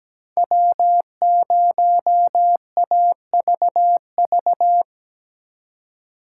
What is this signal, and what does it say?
Morse code, heard as a keyed tone, about 20 wpm, 700 Hz. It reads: W0AVV
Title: Speed Racing - USA 1x3 Call Signs 17wpm